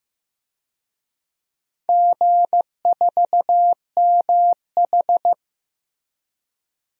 Morse code, heard as a keyed tone, about 15 wpm, 700 Hz. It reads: G4MH
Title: DXCC-100 Call Signs - 4 Characters - Repeated 2x - 15wpm